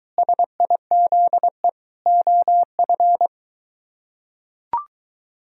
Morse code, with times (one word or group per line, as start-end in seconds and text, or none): 0.2-1.7 SIZE
2.1-3.3 OF